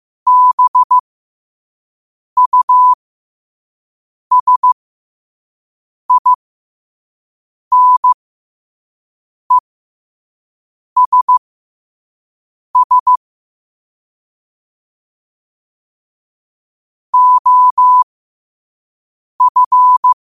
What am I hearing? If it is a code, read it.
Morse code, heard as a keyed tone, about 15 wpm, spaced as for 5 wpm, 1000 Hz. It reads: BUSINESS OF
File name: qotdfiles / 5wpm